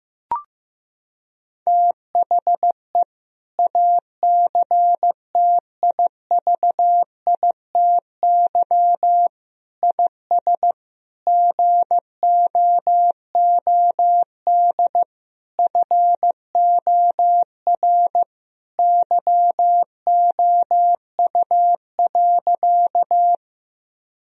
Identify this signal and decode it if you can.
Morse code, heard as a keyed tone, about 15 wpm, 700 Hz. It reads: THE ACTIVITY IS GOOD FOR YOU.
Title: Sentences from Top 1000 Words - Review Part 4 - 15wpm